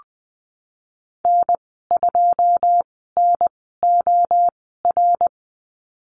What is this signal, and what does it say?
Morse code, heard as a keyed tone, about 20 wpm, spaced as for 15 wpm, 700 Hz. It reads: N2NOR
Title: 088 Lesson 15g - Number 5 - Call Signs 20wpm